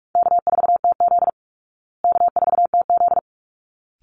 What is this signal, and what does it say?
Morse code, heard as a keyed tone, about 44 wpm, 700 Hz. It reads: K4TZ K4TZ
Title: Call Signs Repeated 3x - 45wpm